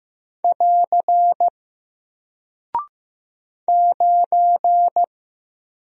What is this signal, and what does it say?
Morse code, heard as a keyed tone, about 15 wpm, 700 Hz. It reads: <AR> 9
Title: Single Character (Letter:Number:Punctuation:Prosign) Alt 15wpm